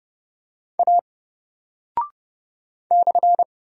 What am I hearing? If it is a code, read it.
Morse code, heard as a keyed tone, about 30 wpm, 700 Hz. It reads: A /
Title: Single Character (Letter:Number:Punctuation:Prosign) Alt 30wpm